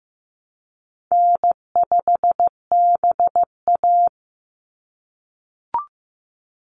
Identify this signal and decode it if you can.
Morse code, heard as a keyed tone, about 15 wpm, 700 Hz. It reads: N5BA
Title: Easy Call Signs - 15wpm